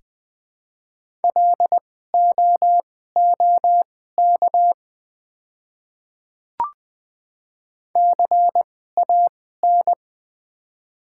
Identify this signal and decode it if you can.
Morse code, heard as a keyed tone, about 20 wpm, spaced as for 15 wpm, 700 Hz. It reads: LOOK CAN
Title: 2x Character Spacing - Top 100 Words - 20wpm